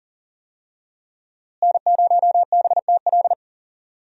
Morse code, heard as a keyed tone, about 37 wpm, 700 Hz. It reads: N0BTL
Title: USA Call Signs - 5 Characters - Repeated 3x - 40wpm